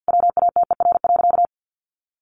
Morse code, heard as a keyed tone, about 50 wpm, 700 Hz. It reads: WATER.
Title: Sentences from Top 500 Words 50wpm